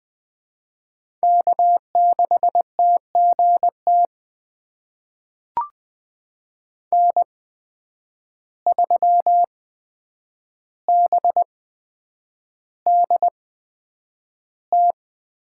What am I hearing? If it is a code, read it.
Morse code, heard as a keyed tone, about 20 wpm, spaced as for 9 wpm, 700 Hz. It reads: K6TGT N3BDT
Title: USA Call Signs - 5 Characters - 8x Character Spacing - 20wpm